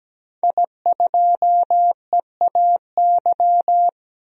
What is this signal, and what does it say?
Morse code, heard as a keyed tone, about 17 wpm, 700 Hz. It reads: I2EAY
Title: DXCC-100 Call Signs - 5 Characters - 17wpm